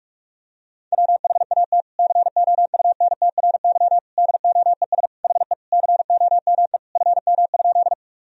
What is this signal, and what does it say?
Morse code, heard as a keyed tone, about 46 wpm, 700 Hz. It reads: WHAT COUNTRY DOES HE COME FM?
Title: Sentences from Top 300 Words and Abbreviations 45wpm